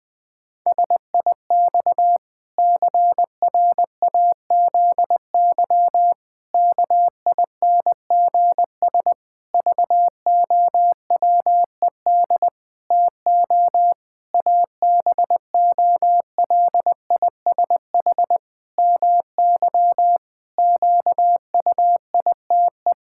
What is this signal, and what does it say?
Morse code, heard as a keyed tone, about 21 wpm, 700 Hz. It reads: SIX CRAZY KINGS VOWED TO ABOLISH MY QUITE
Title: Pangram Challenge v2 - 20wpm